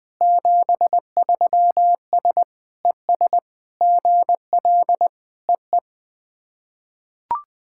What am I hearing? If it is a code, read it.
Morse code, heard as a keyed tone, about 21 wpm, 690 Hz. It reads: 73S ES GL EE
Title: SOTA QSO Practice 20wpm